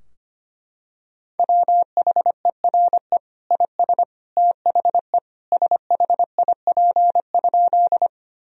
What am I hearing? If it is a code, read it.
Morse code, heard as a keyed tone, about 25 wpm, 700 Hz. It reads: WHERE IS THE SHIP?
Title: Sentences from Top 1000 Words - Review Part 2 - 25wpm